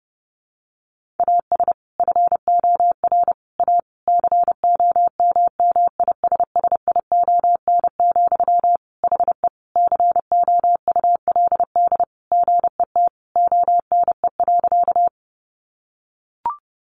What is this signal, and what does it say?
Morse code, heard as a keyed tone, about 30 wpm, 700 Hz. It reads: AS FOR A COMMISSION, HE COULD GET ONE.
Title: Sentences from Top 1400 Words 30wpm